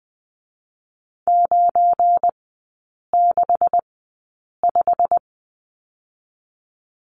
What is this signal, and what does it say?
Morse code, heard as a keyed tone, about 20 wpm, spaced as for 8 wpm, 700 Hz. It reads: 965
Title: 218 Lesson 35e - Number 0 - Three Characters 20wpm